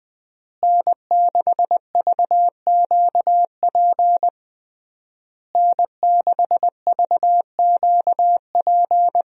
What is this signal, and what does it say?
Morse code, heard as a keyed tone, about 20 wpm, 700 Hz. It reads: N6VQP N6VQP
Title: USA Call Signs - 5 Characters - Repeated 3x - 20wpm